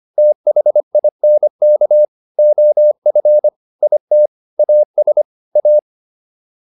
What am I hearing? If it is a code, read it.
Morse code, heard as a keyed tone, about 25 wpm, 600 Hz. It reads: THINK OF IT AS A